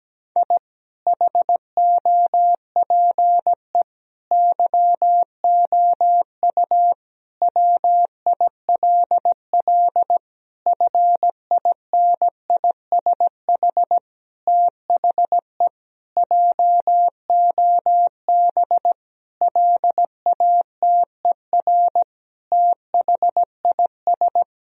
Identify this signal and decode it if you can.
Morse code, heard as a keyed tone, about 17 wpm, 700 Hz. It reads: I HOPE YOU WILL FINISH THE JOB LATER THIS
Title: Sentences from Top 1000 Words - Review Part 3 - 17wpm